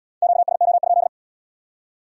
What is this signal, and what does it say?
Morse code, heard as a keyed tone, about 66 wpm, 700 Hz. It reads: QSO?